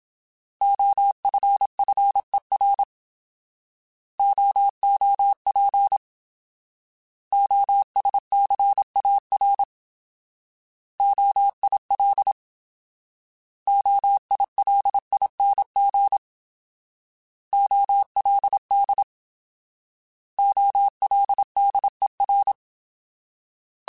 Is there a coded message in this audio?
Yes: Morse code from 0.6 to 22.5 s, heard as a keyed tone, about 27 wpm, 800 Hz. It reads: OFFER OOP OSCAR OIL OILING OLD OLDER